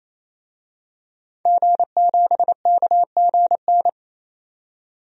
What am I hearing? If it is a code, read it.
Morse code, heard as a keyed tone, about 28 wpm, 700 Hz. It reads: G7KGN
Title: DXCC-100 Call Signs - 5 Characters - 28wpm